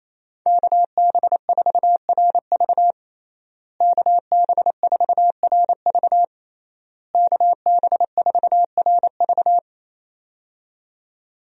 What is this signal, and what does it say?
Morse code, heard as a keyed tone, about 27 wpm, 700 Hz. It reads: KB4RV KB4RV KB4RV
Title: USA Call Signs - 5 Characters - Repeated 3x - 28wpm